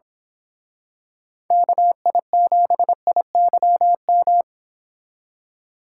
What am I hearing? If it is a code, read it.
Morse code, heard as a keyed tone, about 27 wpm, 700 Hz. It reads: KI7IYM
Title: Speed Racing - USA Call Signs 22wpm